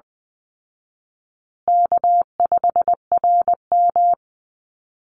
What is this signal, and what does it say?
Morse code, heard as a keyed tone, about 20 wpm, 700 Hz. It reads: K5RM
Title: Easy Call Signs Repeated 2x - 20wpm